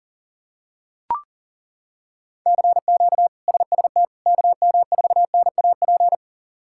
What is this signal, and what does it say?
Morse code, heard as a keyed tone, about 40 wpm, 700 Hz. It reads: CQ SST KM4NAP
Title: K1USN SST QSO Practice - 40wpm